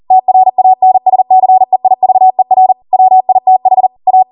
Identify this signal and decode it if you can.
Morse code, heard as a keyed tone, about 40 wpm, 750 Hz. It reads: TRANSCEIVER WITH A